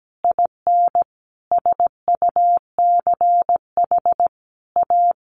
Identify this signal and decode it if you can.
Morse code, heard as a keyed tone, about 17 wpm, 700 Hz. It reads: IN SUCH A
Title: Sentences from Top 1200 Words 17wpm